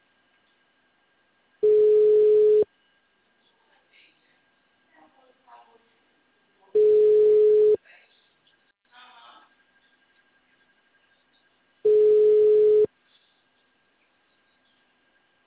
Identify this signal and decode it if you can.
Ringback tone (Europe, 425 Hz), with speech.